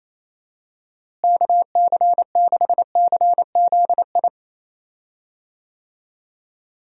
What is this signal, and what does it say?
Morse code, heard as a keyed tone, about 28 wpm, 700 Hz. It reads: KC6CZI
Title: USA Call Signs - 6 Characters - Repeated 2x - 28wpm